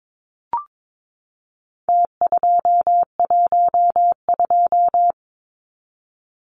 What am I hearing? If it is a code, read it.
Morse code, heard as a keyed tone, about 22 wpm, 700 Hz. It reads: T212
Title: Numbers - Contest Style - 4-Digit 22wpm